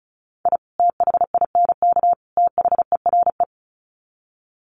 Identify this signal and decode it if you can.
Morse code, heard as a keyed tone, about 34 wpm, 700 Hz. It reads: I THINK THERE
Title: Sets of 3 Words 35wpm